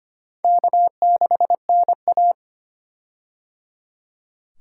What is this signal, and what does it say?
Morse code, heard as a keyed tone, about 25 wpm, 700 Hz. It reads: K6NA
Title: K1USN SST QSO Practice - 25wpm